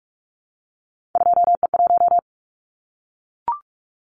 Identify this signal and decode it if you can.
Morse code, heard as a keyed tone, about 44 wpm, 700 Hz. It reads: 2E1
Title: Callsign Prefixes for Top 25 DXCC - Repeated 3x - 45wpm